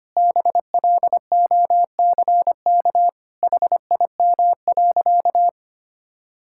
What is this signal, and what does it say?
Morse code, heard as a keyed tone, about 25 wpm, 700 Hz. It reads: BLOCK HIM.
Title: Sentences from Top 1600 Words 25wpm